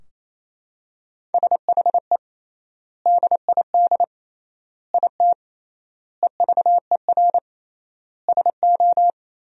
Morse code, heard as a keyed tone, about 28 wpm, 700 Hz. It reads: SHE DID IT EVER SO
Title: 3x Word-Spacing - Sentences from Top 400 Words - 28wpm